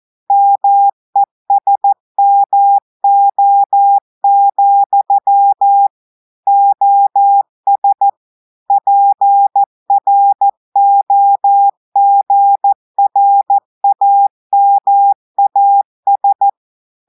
Morse code, heard as a keyed tone, about 14 wpm, 800 Hz. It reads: MESMO, OS PROGRAMAS
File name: audios / portugais / portugais texte en clair 2 vitesse 14 mots minutes